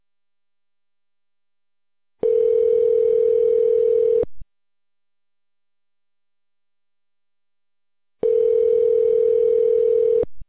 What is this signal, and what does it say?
Ringback tone (North America), 440 + 480 Hz.